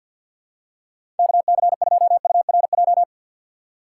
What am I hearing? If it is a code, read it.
Morse code, heard as a keyed tone, about 51 wpm, 700 Hz. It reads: KC2URJ
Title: USA Call Signs - 3x Character Spacing - 50wpm